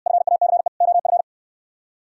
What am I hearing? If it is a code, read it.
Morse code, heard as a keyed tone, about 66 wpm, 700 Hz. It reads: FACE OF